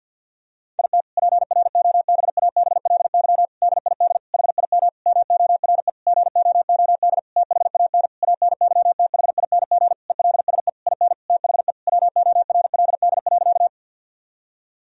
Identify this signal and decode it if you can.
Morse code, heard as a keyed tone, about 50 wpm, 700 Hz. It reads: IT PROBABLY DID HIM MORE GOOD THAN ANYTHING ELSE IN THE WORLD.
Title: Sentences from Top 1000 Words - Review Part 4 - 50wpm